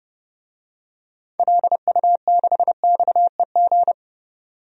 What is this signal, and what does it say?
Morse code, heard as a keyed tone, about 29 wpm, 700 Hz. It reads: LU6XEG
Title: DXCC-100 Call Signs - 6 Characters - 30wpm